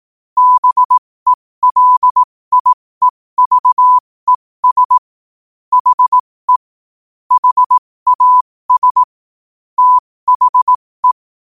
Morse code, heard as a keyed tone, about 18 wpm, spaced as for 15 wpm, 1000 Hz. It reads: BELIEVES HE HAS THE
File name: qotdfiles / 15wpm